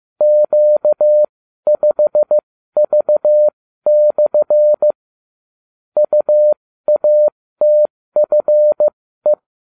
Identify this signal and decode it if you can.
Morse code, heard as a keyed tone, about 15 wpm, spaced as for 13 wpm, 600 Hz. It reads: Q5V/ UATFE